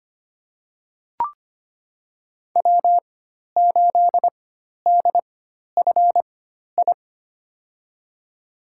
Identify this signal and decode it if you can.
Morse code, heard as a keyed tone, about 25 wpm, spaced as for 12 wpm, 700 Hz. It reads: W8DFI
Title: USA Call Signs - 5 Characters - 4x Character Spacing - 25wpm